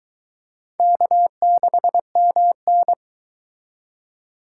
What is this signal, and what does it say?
Morse code, heard as a keyed tone, about 22 wpm, 700 Hz. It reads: K6MN